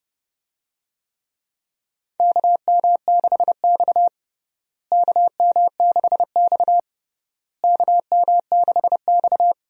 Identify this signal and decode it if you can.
Morse code, heard as a keyed tone, about 31 wpm, 700 Hz. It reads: KM6X KM6X KM6X
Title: USA Call Signs - 4 Characters - Repeated 3x - 30wpm